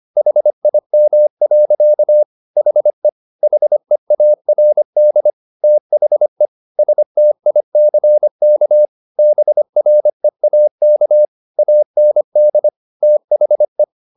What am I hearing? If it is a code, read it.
Morse code, heard as a keyed tone, about 25 wpm, 600 Hz. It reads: HIM. HE HEARD THE STICK BREAK AND THE
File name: the old man and the sea part 1 25wpm